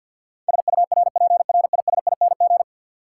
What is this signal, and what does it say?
Morse code, heard as a keyed tone, about 52 wpm, 700 Hz. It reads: SURPRISING
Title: Sets of 5 Words - New Word Drill - 50wpm